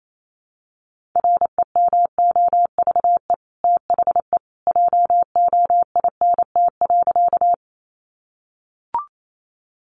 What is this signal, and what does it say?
Morse code, heard as a keyed tone, about 28 wpm, 700 Hz. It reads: REMOVE THE JOINT.
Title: Sentences from Top 1300 Words 28wpm